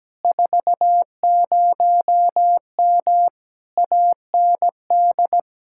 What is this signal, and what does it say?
Morse code, heard as a keyed tone, about 17 wpm, 700 Hz. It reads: 40M AND
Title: Speed-Racing Common QSO Phrases - 17wpm